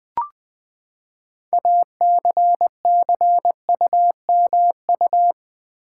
Morse code, heard as a keyed tone, about 20 wpm, 700 Hz. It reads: ACCUMU
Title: Build A Word - 6 Letters At A Time - 12-Letter Words - 20wpm